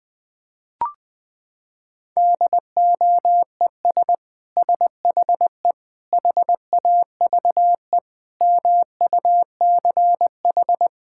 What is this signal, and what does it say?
Morse code, heard as a keyed tone, about 20 wpm, 700 Hz. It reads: DOES SHE HAVE MUCH
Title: Sentences from Top 1000 Words - Review Part 6 - 20wpm